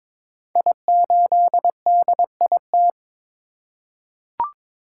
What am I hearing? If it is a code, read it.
Morse code, heard as a keyed tone, about 22 wpm, 690 Hz. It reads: I8DIT
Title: Code Groups - 5 Letter:Numbers - 22wpm